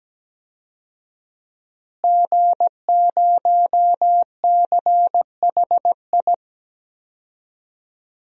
Morse code, heard as a keyed tone, about 17 wpm, 700 Hz. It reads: G0CHI
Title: DXCC-100 Call Signs - 5 Characters - Repeated 2x - 17wpm